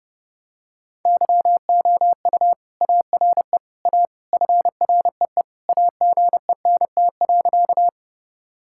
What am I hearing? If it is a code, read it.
Morse code, heard as a keyed tone, about 28 wpm, 700 Hz. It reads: YOU ARE A FREE AGENT.
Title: Sentences from Top 1600 Words 30wpm